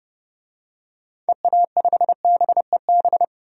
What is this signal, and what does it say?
Morse code, heard as a keyed tone, about 32 wpm, 700 Hz. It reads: EA5BEB
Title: DXCC-100 Call Signs - 6 Characters - 30wpm